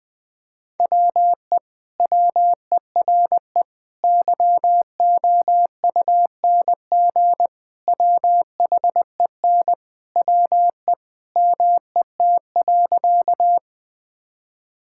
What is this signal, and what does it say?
Morse code, heard as a keyed tone, about 21 wpm, 700 Hz. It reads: WE WERE YOUNG WHEN WE MET.
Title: Sentences with 2-Word Phrases - Spoken Beginning - 20wpm